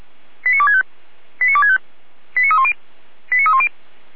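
Five-tone selective call (ZVEI), heard as quick runs of five tones, tones of about 80 ms. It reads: 89265 89265 89310 89310
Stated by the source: ZVEI five-tone selective call received on SDR.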